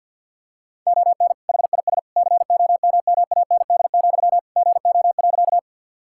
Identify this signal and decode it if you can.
Morse code, heard as a keyed tone, about 51 wpm, 700 Hz. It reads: ON HIS COMMAND, GO.
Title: Sentences from Top 1600 Words 50wpm